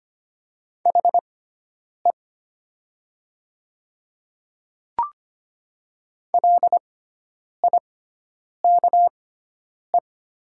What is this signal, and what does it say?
Morse code, heard as a keyed tone, about 25 wpm, spaced as for 9 wpm, 700 Hz. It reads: HE LIKE
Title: 6x Character Spacing - Top 100 Words - 25wpm